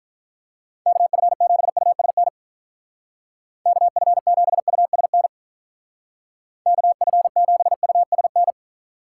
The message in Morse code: KF7USN KF7USN KF7USN

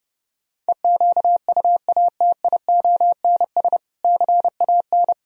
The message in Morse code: EQUATIONS CAN

